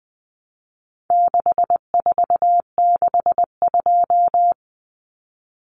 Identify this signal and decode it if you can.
Morse code, heard as a keyed tone, about 20 wpm, 700 Hz. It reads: 6462
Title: Numbers - 4-Digit 20wpm